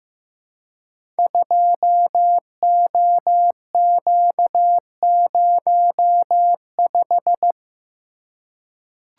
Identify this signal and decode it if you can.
Morse code, heard as a keyed tone, about 15 wpm, 700 Hz. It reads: 2OQ05
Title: Code Groups - 5 Letter:Numbers - 15wpm